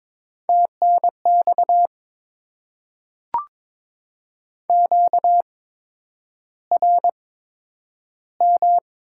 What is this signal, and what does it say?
Morse code, heard as a keyed tone, about 22 wpm, spaced as for 9 wpm, 700 Hz. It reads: TNX QRM
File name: Top 90 QSO Elements - 8x Character Spacing - 22wpm